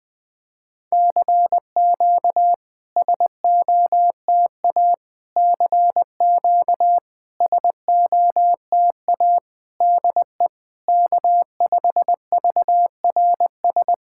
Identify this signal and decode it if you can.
Morse code, heard as a keyed tone, about 20 wpm, 690 Hz. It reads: CQ SOTA CQ SOTA DE K5VRS
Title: SOTA QSO Practice 20wpm